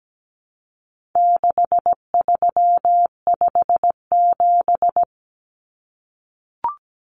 Morse code, heard as a keyed tone, about 17 wpm, 700 Hz. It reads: 6357
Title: Common Cut Numbers - 4-Digit 17wpm